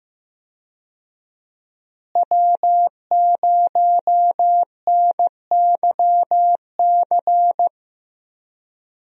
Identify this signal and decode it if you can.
Morse code, heard as a keyed tone, about 15 wpm, 700 Hz. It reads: W0NYC
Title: USA Call Signs - 5 Characters - Repeated 2x - 15wpm